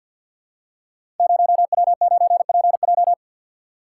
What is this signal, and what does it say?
Morse code, heard as a keyed tone, about 47 wpm, 700 Hz. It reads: 0W9PJ
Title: Code Groups - 5 Letter:Numbers - 50wpm